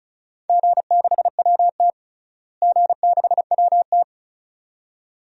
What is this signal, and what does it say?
Morse code, heard as a keyed tone, about 36 wpm, 700 Hz. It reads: G6WT G6WT
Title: DXCC-100 Call Signs - 4 Characters - Repeated 2x - 35wpm